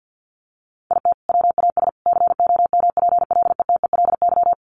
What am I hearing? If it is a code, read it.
Morse code, heard as a keyed tone, about 50 wpm, 700 Hz. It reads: IT WAS COMPLETELY
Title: Sentences from Top 1100 Words 50wpm